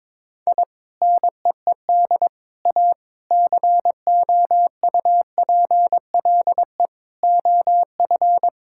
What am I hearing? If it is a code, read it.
Morse code, heard as a keyed tone, about 22 wpm, 700 Hz. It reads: I NEED A COUPLE OF
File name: practice / 22wpm / Sentences with 3-Word Phrases - 22wpm